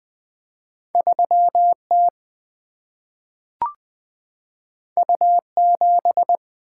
Morse code, heard as a keyed tone, about 20 wpm, 700 Hz. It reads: 3T U7
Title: Single-Letter Single-Number 20wpm